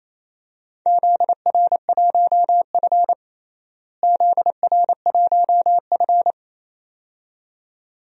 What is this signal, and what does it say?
Morse code, heard as a keyed tone, about 27 wpm, 700 Hz. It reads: ZR1F ZR1F
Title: DXCC-100 Call Signs - 4 Characters - Repeated 2x - 28wpm